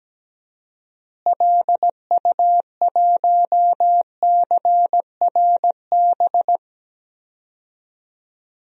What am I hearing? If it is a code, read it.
Morse code, heard as a keyed tone, about 17 wpm, 700 Hz. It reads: LU1CRB
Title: DXCC-100 Call Signs - 6 Characters - Repeated 2x - 17wpm